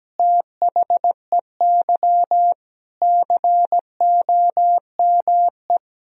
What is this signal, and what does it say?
Morse code, heard as a keyed tone, about 17 wpm, 700 Hz. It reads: THEY COME
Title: Sentences from Top 200 Words and Abbreviations 17wpm